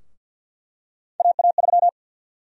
Morse code, heard as a keyed tone, about 51 wpm, 700 Hz. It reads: AA3
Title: USA Call Sign Prefixes - Repeated 2x - 50wpm